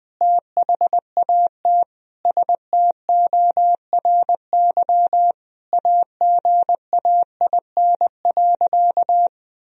Morse code, heard as a keyed tone, about 20 wpm, 700 Hz. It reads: THAT STORY AGAIN.
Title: Sentences from Top 1000 Words - Review Part 1 - 20wpm